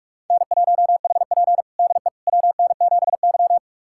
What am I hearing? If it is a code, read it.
Morse code, heard as a keyed tone, about 46 wpm, 700 Hz. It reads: N1HP DE WN7Y
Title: USA Callsign DE Callsign - 4 Characters - 45wpm